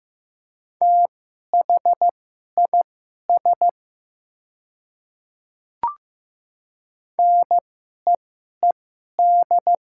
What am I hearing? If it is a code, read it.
Morse code, heard as a keyed tone, about 15 wpm, spaced as for 11 wpm, 700 Hz. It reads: THIS NEED
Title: 2x Character Spacing - Top 200 Words - 15wpm